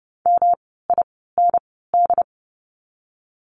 Morse code, heard as a keyed tone, about 30 wpm, spaced as for 17 wpm, 700 Hz. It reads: MIND
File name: course / 30:10wpm / 108 Lesson 18f - Letter m - Words 30wpm